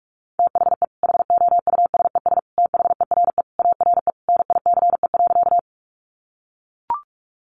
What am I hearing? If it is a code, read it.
Morse code, heard as a keyed tone, about 44 wpm, 700 Hz. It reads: THE HOUSES THERE ARE NICE.
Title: Easy Phrases 45wpm